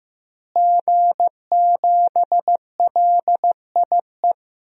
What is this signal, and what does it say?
Morse code, heard as a keyed tone, about 15 wpm, 700 Hz. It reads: G7LIE
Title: DXCC-100 Call Signs - 5 Characters - Repeated 2x - 15wpm